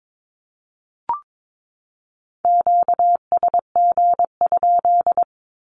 Morse code, heard as a keyed tone, about 22 wpm, 700 Hz. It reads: QSG?
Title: Q-Codes 22wpm